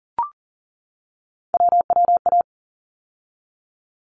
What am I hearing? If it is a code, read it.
Morse code, heard as a keyed tone, about 40 wpm, 700 Hz. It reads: WWA
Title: ARRL Sections 40wpm